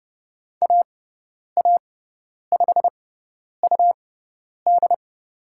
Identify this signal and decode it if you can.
Morse code, heard as a keyed tone, about 30 wpm, spaced as for 10 wpm, 700 Hz. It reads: AA5UD
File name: Lesson 17g - Letter u - Call Signs 30wpm